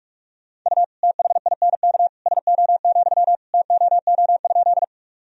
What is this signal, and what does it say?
Morse code, heard as a keyed tone, about 42 wpm, 700 Hz. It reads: U THINK SO, TOO?
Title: Sentences from Top 200 Words and Abbreviations 45wpm